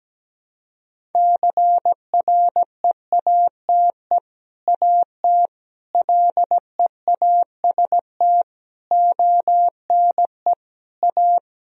Morse code, heard as a keyed tone, about 17 wpm, 700 Hz. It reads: CREATE AT LEAST ONE A
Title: Sentences from Top 1000 Words - Review Part 3 - 17wpm